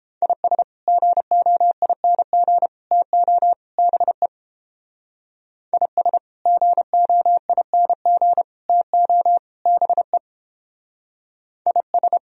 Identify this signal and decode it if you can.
Morse code, heard as a keyed tone, about 29 wpm, 700 Hz. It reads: IS GOING TO BE IS GOING TO BE IS